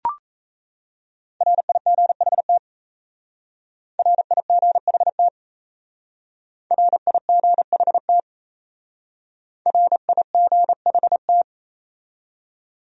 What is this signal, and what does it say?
Morse code, heard as a keyed tone, about 37 wpm, 700 Hz. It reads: RIGHT RIGHT RIGHT RIGHT